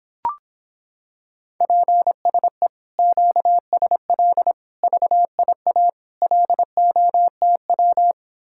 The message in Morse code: PSE QSL VIA LOTW

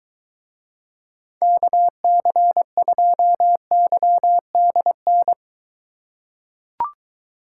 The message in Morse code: KC2YDN

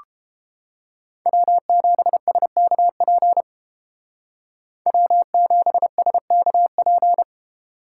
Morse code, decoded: W7SKP W7SKP

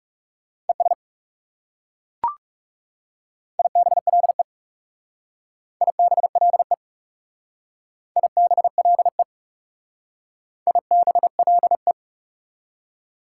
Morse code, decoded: ES IBLE IBLE IBLE IBLE

